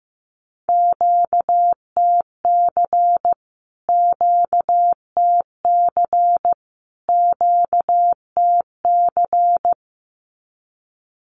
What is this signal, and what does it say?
Morse code, heard as a keyed tone, about 15 wpm, 700 Hz. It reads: QTC QTC QTC
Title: Q-Codes Repeated 3x 15wpm